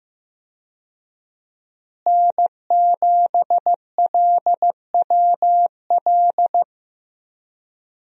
Morse code, heard as a keyed tone, about 15 wpm, 700 Hz. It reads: N7LWL
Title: USA Call Signs - 5 Characters - Repeated 3x - 15wpm